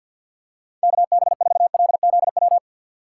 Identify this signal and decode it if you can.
Morse code, heard as a keyed tone, about 52 wpm, 690 Hz. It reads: KB4LZW
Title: USA Call Signs - 3x Character Spacing - 50wpm